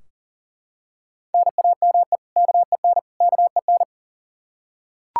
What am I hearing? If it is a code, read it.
Morse code, heard as a keyed tone, about 40 wpm, 700 Hz. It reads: NAME KEN KEN
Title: Practice QSO - Exchange 1 - Beginning - 40wpm